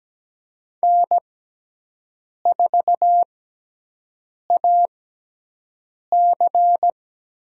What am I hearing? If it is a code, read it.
Morse code, heard as a keyed tone, about 17 wpm, spaced as for 6 wpm, 700 Hz. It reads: N4AC
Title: USA Call Signs - 6x Character Spacing - 17wpm